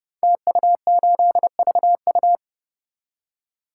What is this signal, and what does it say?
Morse code, heard as a keyed tone, about 29 wpm, 700 Hz. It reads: TU8VU
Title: Code Groups - 5 Letter:Numbers - 30wpm